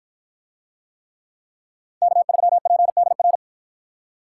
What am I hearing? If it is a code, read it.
Morse code, heard as a keyed tone, about 53 wpm, 700 Hz. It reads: K3PDR